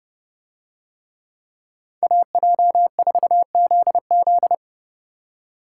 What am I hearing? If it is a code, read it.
Morse code, heard as a keyed tone, about 30 wpm, 700 Hz. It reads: AJ4ZZ